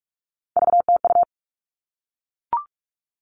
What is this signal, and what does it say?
Morse code, heard as a keyed tone, about 44 wpm, 700 Hz. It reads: VTU
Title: All Cut Numbers - 3-Digit 45wpm